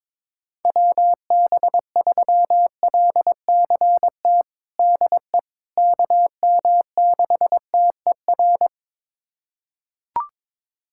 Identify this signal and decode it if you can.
Morse code, heard as a keyed tone, about 21 wpm, 700 Hz. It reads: WB3LCT DE KM6TER
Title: USA Callsign DE Callsign 22wpm